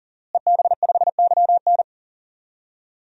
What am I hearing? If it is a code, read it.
Morse code, heard as a keyed tone, about 40 wpm, 700 Hz. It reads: EB5YN
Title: DXCC-100 Call Signs - 5 Characters - Repeated 2x - 40wpm